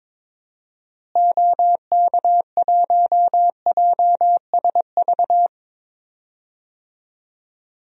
Morse code, heard as a keyed tone, about 22 wpm, 700 Hz. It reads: OK1JSV
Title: DXCC-100 Call Signs - 6 Characters - Repeated 2x - 22wpm